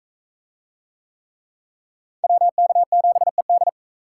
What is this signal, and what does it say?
Morse code, heard as a keyed tone, about 38 wpm, 700 Hz. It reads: WK7ED